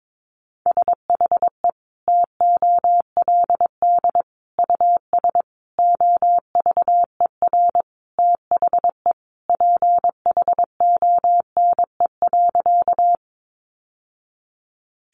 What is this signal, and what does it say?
Morse code, heard as a keyed tone, about 22 wpm, 700 Hz. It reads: SHE TOLD US OVER THE PHONE.
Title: Sentences from Top 1100 Words 22wpm